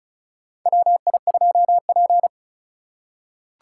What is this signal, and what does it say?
Morse code, heard as a keyed tone, about 36 wpm, 700 Hz. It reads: WI2P
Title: USA Call Signs - 4 Characters - 4x Character Spacing - 35wpm